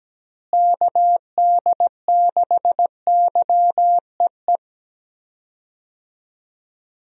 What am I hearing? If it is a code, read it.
Morse code, heard as a keyed tone, about 17 wpm, 700 Hz. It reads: KD6YEE
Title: USA Call Signs - 6 Characters - Repeated 3x - 17wpm